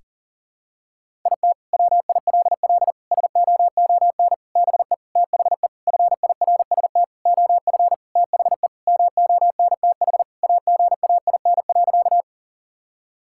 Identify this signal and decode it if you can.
Morse code, heard as a keyed tone, about 40 wpm, 700 Hz. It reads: IT WILL SOON BE THE FIRST OF THE MONTH AGAIN.
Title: Sentences from Top 1000 Words - Review Part 5 - 40wpm